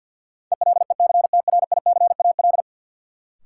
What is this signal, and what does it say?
Morse code, heard as a keyed tone, about 51 wpm, 700 Hz. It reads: ELECTRICAL
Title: Sentences from Top 1000 Words - Review Part 6 - New Word Drill - 50wpm